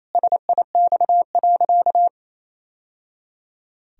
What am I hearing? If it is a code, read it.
Morse code, heard as a keyed tone, about 27 wpm, 700 Hz. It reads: SIX.
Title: Sentences from Top 1000 Words - Review Part 2 - 28wpm